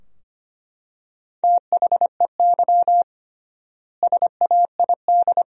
Morse code, heard as a keyed tone, about 25 wpm, 700 Hz. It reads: THEY SAID
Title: 3x Word-Spacing - Sentences from Top 100 Words - 25wpm